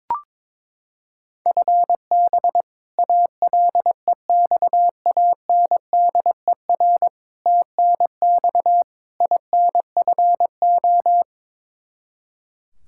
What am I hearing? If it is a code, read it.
Morse code, heard as a keyed tone, about 21 wpm, 700 Hz. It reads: FB ALEXANDER TNX INFO
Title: K1USN SST QSO Practice 22wpm